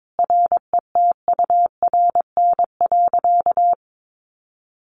Morse code, heard as a keyed tone, about 22 wpm, 700 Hz. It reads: RETURN.
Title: A Sound Of Thunder 22wpm